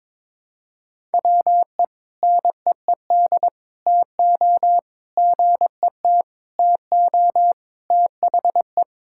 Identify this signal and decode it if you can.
Morse code, heard as a keyed tone, about 22 wpm, 700 Hz. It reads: WE NEED TO GET TO THE
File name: practice / 22wpm / Sentences with 3-Word Phrases - 22wpm